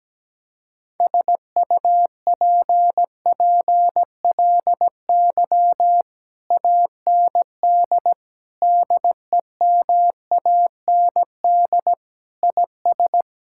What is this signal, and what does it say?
Morse code, heard as a keyed tone, about 17 wpm, 700 Hz. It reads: SUPPLY AND DEMAND IS
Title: Sentences from Top 1000 Words - Review Part 6 - 17wpm